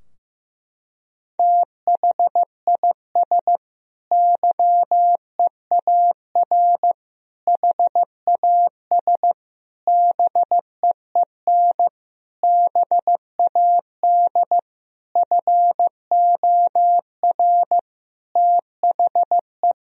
Morse code, heard as a keyed tone, about 15 wpm, 700 Hz. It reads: THIS YEAR HAS BEEN BAD FOR THE
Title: Sentences from Top 1000 Words - Review Part 3 - 15wpm